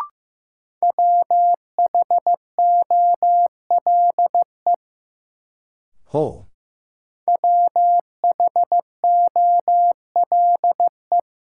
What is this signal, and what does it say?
Morse code, heard as a keyed tone, about 15 wpm, 700 Hz. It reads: WHOLE WHOLE
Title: Sentences from Top 400 Words - New Word Drill - 15wpm